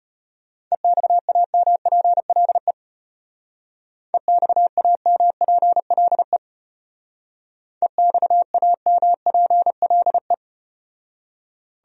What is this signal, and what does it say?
Morse code, heard as a keyed tone, about 34 wpm, 700 Hz. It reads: EXAMPLE EXAMPLE EXAMPLE